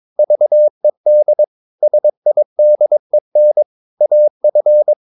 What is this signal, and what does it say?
Morse code, heard as a keyed tone, about 22 wpm, 600 Hz. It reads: VED SIDEN AF
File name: Ferie QRS 22wpm